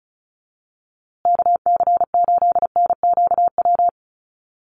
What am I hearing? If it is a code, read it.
Morse code, heard as a keyed tone, about 34 wpm, 700 Hz. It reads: KC8NQW
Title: USA Call Signs Repeated 3x 35wpm